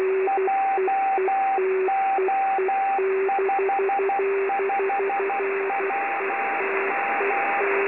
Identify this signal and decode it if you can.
Morse code, heard as a keyed tone, about 12 wpm, 790 Hz, with radio noise. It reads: JO55WM
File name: extra / websdr recording .3kHz OZ7IGY